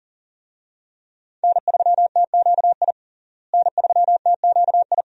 Morse code, heard as a keyed tone, about 40 wpm, 700 Hz. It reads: N3TQI N3TQI
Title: USA Call Signs - 5 Characters - Repeated 2x - 40wpm